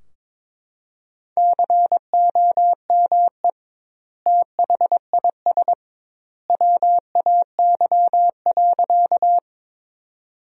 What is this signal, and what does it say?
Morse code, heard as a keyed tone, about 21 wpm, 700 Hz. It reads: COME THIS WAY.
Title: 2x Word-Spacing - Sentences from Top 100 Words - 22wpm